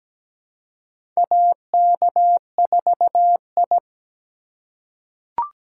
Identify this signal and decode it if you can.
Morse code, heard as a keyed tone, about 17 wpm, 700 Hz. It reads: AK4I
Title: USA Call Signs - 4 Characters - 2x Character Spacing - 17wpm